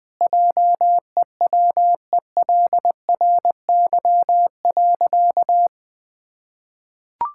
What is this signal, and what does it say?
Morse code, heard as a keyed tone, about 21 wpm, 700 Hz. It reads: JEWELRY.
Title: Pangram Challenge v2 - 20wpm